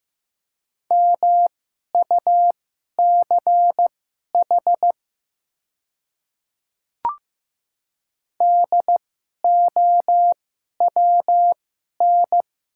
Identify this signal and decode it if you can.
Morse code, heard as a keyed tone, about 15 wpm, spaced as for 11 wpm, 700 Hz. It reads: MUCH DOWN
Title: 2x Character Spacing - Top 300 Words - 15wpm